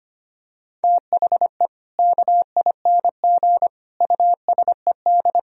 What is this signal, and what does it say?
Morse code, heard as a keyed tone, about 25 wpm, 700 Hz. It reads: THE KING USED
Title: Sentences from Top 1000 Words - Review Part 4 - 25wpm